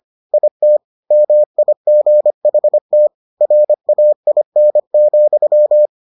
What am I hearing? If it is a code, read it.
Morse code, heard as a keyed tone, about 25 wpm, 600 Hz. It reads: IT MIGHT RAIN,